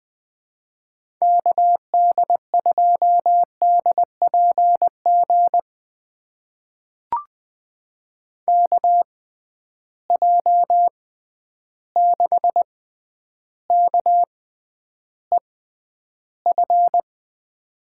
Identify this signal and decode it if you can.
Morse code, heard as a keyed tone, about 20 wpm, spaced as for 10 wpm, 700 Hz. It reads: KD2DPG KJ6KEF
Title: USA Call Signs - 6 Characters - 6x Character Spacing - 20wpm